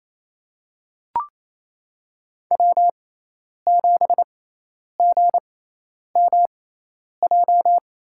Morse code, heard as a keyed tone, about 28 wpm, spaced as for 10 wpm, 700 Hz. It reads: W7GMJ
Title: USA Call Signs - 5 Characters - 6x Character Spacing - 28wpm